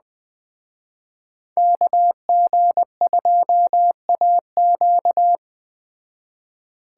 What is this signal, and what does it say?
Morse code, heard as a keyed tone, about 20 wpm, 700 Hz. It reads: KG2AQ
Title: Speed Racing - USA Call Signs 17wpm